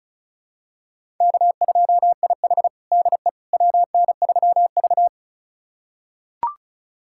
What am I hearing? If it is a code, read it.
Morse code, heard as a keyed tone, about 36 wpm, 700 Hz. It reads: K2IH DE WN3V
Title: USA Callsign DE Callsign - 4 Characters - 35wpm